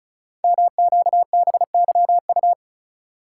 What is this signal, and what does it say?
Morse code, heard as a keyed tone, about 36 wpm, 700 Hz. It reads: MQBYU
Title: Code Groups - 5 Letters - 35wpm